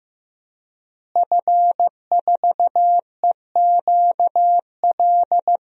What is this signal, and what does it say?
Morse code, heard as a keyed tone, about 15 wpm, 700 Hz. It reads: F4EQL
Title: DXCC-100 Call Signs - 5 Characters - Repeated 3x - 15wpm